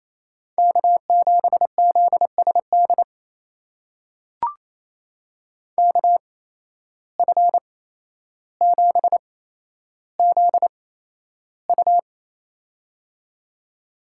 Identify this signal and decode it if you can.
Morse code, heard as a keyed tone, about 28 wpm, spaced as for 12 wpm, 700 Hz. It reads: K7ZSD KF7ZU
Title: USA Call Signs - 5 Characters - 8x Character Spacing - 28wpm